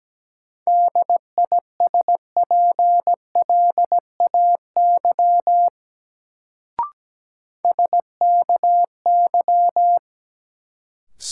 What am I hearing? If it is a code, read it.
Morse code, heard as a keyed tone, about 17 wpm, 700 Hz. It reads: DISPLAY SKY
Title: Sentences from Top 1200 Words - New Word Drill - 17wpm